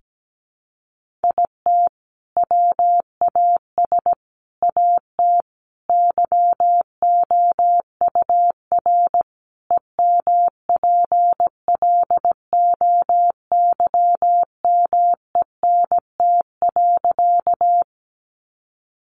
Morse code, heard as a keyed tone, about 17 wpm, 700 Hz. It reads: IT WAS AT YOUR EMPLOYMENT.
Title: Sentences from Top 1500 Words 17wpm